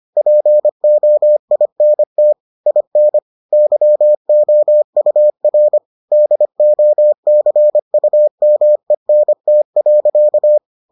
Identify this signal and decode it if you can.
Morse code, heard as a keyed tone, about 25 wpm, 600 Hz. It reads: POINT IN YOUR DOCUMENT.